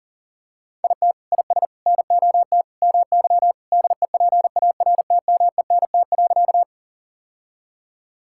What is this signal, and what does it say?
Morse code, heard as a keyed tone, about 37 wpm, 700 Hz. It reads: IT IS NOT MY DEPARTMENT.
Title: Sentences from Top 1000 Words - Review Part 5 - 40wpm